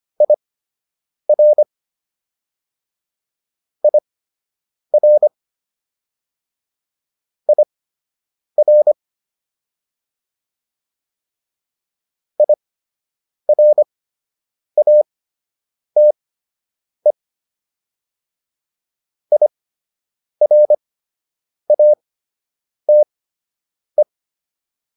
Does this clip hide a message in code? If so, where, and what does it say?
Yes: Morse code from 0.2 to 24.0 s, heard as a keyed tone, about 25 wpm, spaced as for 8 wpm, 600 Hz. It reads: IR IR IR IRATE IRATE